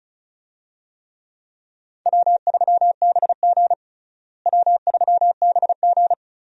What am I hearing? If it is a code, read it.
Morse code, heard as a keyed tone, about 36 wpm, 700 Hz. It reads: W3BG W3BG
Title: USA Call Signs - 4 Characters - Repeated 2x - 35wpm